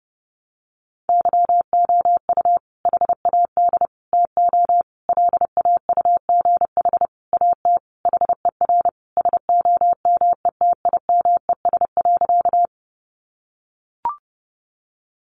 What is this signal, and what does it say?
Morse code, heard as a keyed tone, about 30 wpm, 700 Hz. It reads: YOU HAD TO LAUGH AT HER SOMETIMES.
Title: Sentences from Top 600 Words 30wpm